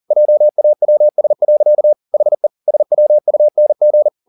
Morse code, heard as a keyed tone, about 40 wpm, 600 Hz. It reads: JAWS. HE SWUNG
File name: the old man and the sea 2 40wpm